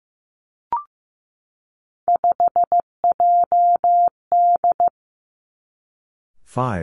Morse code, heard as a keyed tone, about 15 wpm, 700 Hz. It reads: 5JD
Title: Three Challenging Letter-Number Combinations 15wpm